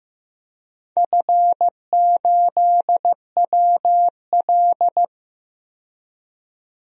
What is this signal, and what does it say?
Morse code, heard as a keyed tone, about 15 wpm, 700 Hz. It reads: F8WL
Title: DXCC-100 Call Signs - 4 Characters - 15wpm